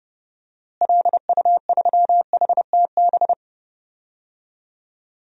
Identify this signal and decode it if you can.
Morse code, heard as a keyed tone, about 30 wpm, 700 Hz. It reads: LU3HTB
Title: DXCC-100 Call Signs - 6 Characters - Repeated 3x - 30wpm